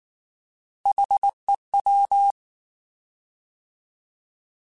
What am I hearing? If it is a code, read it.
Morse code, heard as a keyed tone, about 19 wpm, 800 Hz. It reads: HEW